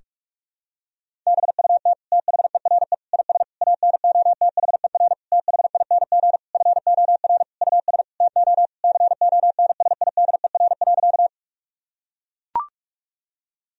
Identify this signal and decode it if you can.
Morse code, heard as a keyed tone, about 43 wpm, 700 Hz. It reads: BUT THERE IS ANOTHER THING FOR US TO CONSIDER.
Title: Sentences from Top 1000 Words - Review Part 4 - 45wpm